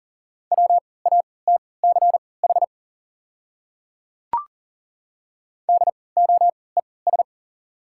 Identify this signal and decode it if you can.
Morse code, heard as a keyed tone, about 40 wpm, spaced as for 23 wpm, 690 Hz. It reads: WATCH DOES
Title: 3x Character Spacing - Top 500 Words - 40wpm